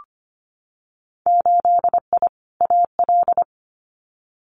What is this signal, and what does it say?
Morse code, heard as a keyed tone, about 25 wpm, 700 Hz. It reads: 8I AL
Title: Winter Field Day Exchanges Encore 25wpm